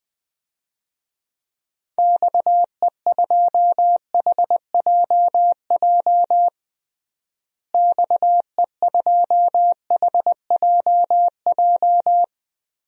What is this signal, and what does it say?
Morse code, heard as a keyed tone, about 20 wpm, 700 Hz. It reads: XE2HJJ XE2HJJ
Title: DXCC-100 Call Signs - 6 Characters - Repeated 3x - 20wpm